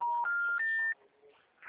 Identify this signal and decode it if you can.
Special information tone.